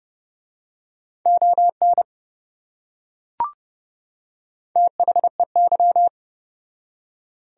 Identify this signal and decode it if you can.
Morse code, heard as a keyed tone, about 27 wpm, 700 Hz. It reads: ON THEY